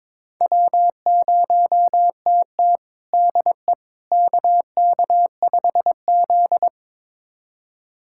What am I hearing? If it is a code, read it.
Morse code, heard as a keyed tone, about 21 wpm, 700 Hz. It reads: W0TT DE KK5Z
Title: USA Callsign DE Callsign - 4 Characters - 22wpm